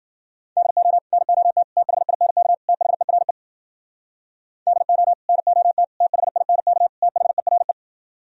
Morse code, heard as a keyed tone, about 52 wpm, 700 Hz. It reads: DO NOT THINK THERE DO NOT THINK THERE